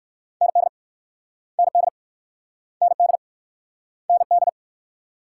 Morse code, heard as a keyed tone, about 58 wpm, 690 Hz. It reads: ND ND ND ND